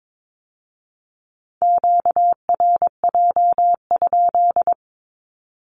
Morse code, heard as a keyed tone, about 22 wpm, 700 Hz. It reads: QRJ?
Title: Q-Codes with Spoken Meaning - 22wpm